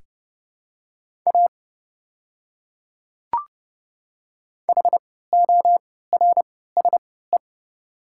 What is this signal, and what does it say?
Morse code, heard as a keyed tone, about 30 wpm, spaced as for 17 wpm, 700 Hz. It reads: A HORSE
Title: 3x Character Spacing - Top 300 Words - 30wpm